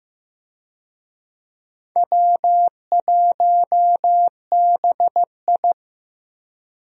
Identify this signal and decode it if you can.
Morse code, heard as a keyed tone, about 15 wpm, 700 Hz. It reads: W1BI
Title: USA Call Signs - 4 Characters - Repeated 3x - 15wpm